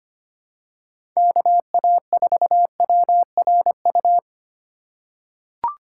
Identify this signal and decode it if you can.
Morse code, heard as a keyed tone, about 25 wpm, 700 Hz. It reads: KA4WRU
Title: USA Call Signs - 6 Characters - 2x Character Spacing - 25wpm